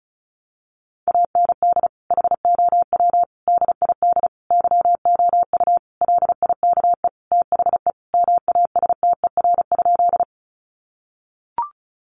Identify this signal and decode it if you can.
Morse code, heard as a keyed tone, about 34 wpm, 700 Hz. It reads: AND HOW DID YOU LIKE THE MASTER?
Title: Sentences from Top 1300 Words 35wpm